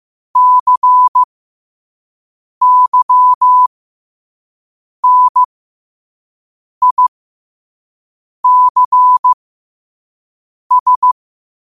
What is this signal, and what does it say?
Morse code, heard as a keyed tone, about 15 wpm, spaced as for 5 wpm, 1000 Hz. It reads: CYNICS
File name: qotdfiles / 5wpm